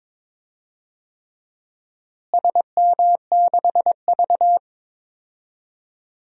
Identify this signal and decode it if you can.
Morse code, heard as a keyed tone, about 22 wpm, 700 Hz. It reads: SM6V
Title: DXCC-100 Call Signs - 4 Characters - Repeated 3x - 22wpm